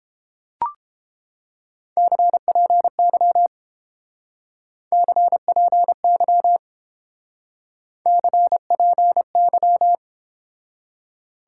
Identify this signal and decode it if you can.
Morse code, heard as a keyed tone, about 29 wpm, 700 Hz. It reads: CPY CPY CPY